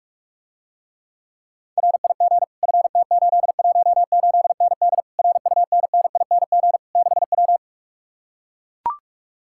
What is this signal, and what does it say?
Morse code, heard as a keyed tone, about 44 wpm, 700 Hz. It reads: RIG FT818ND RUNNING 6W